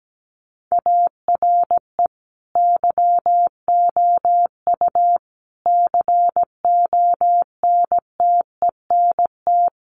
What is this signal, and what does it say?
Morse code, heard as a keyed tone, about 17 wpm, 700 Hz. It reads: ARE YOU CONTENT
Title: Sentences from Top 900 Words 17wpm